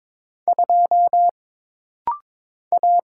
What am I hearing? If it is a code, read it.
Morse code, heard as a keyed tone, about 22 wpm, 700 Hz. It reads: 2 A
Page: This practice set is composed of Letters, Numbers, common punctuations, and common prosigns.